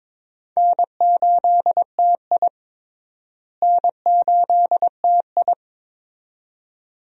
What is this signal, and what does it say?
Morse code, heard as a keyed tone, about 23 wpm, 700 Hz. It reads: N8TI N8TI
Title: USA Call Signs - 4 Characters - Repeated 2x - 22wpm